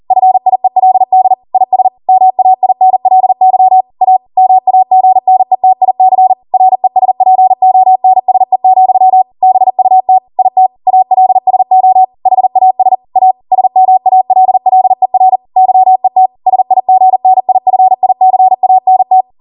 Morse code, heard as a keyed tone, about 40 wpm, 750 Hz. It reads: FIELD IS MAINLY A MAGNETIC RESPONSE, BUT IT ALSO HAS A SMALLER YET SIGNIFICANT
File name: Morse / 40WPM